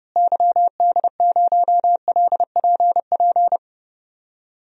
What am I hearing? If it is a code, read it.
Morse code, heard as a keyed tone, about 31 wpm, 700 Hz. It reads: YD0LPP
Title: DXCC-100 Call Signs - 6 Characters - Repeated 2x - 30wpm